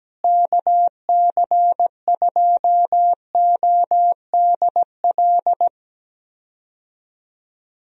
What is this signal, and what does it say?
Morse code, heard as a keyed tone, about 17 wpm, 700 Hz. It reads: KC2ODL
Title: USA Call Signs - 6 Characters - Repeated 2x - 17wpm